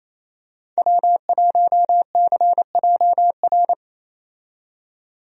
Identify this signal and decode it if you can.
Morse code, heard as a keyed tone, about 28 wpm, 700 Hz. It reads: W1CJR
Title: USA Call Signs - 5 Characters - 28wpm